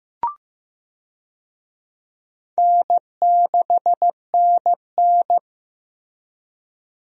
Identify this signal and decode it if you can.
Morse code, heard as a keyed tone, about 15 wpm, 700 Hz. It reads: N6NN
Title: USA Call Signs - 4 Characters - Repeated 2x - 15wpm